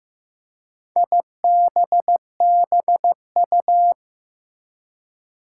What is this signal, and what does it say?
Morse code, heard as a keyed tone, about 15 wpm, 700 Hz. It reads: IBBU
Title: Four Challenging Letter-Number Combinations 15wpm